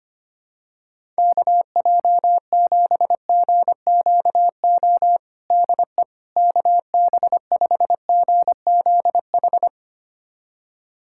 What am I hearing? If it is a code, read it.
Morse code, heard as a keyed tone, about 25 wpm, 700 Hz. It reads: KJ7GQO DE KB5GZH
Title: USA Callsign DE Callsign - 6 Characters - 25wpm